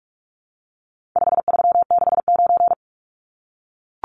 Morse code, heard as a keyed tone, about 44 wpm, 700 Hz. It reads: 5369
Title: Numbers - 4-Digit 45wpm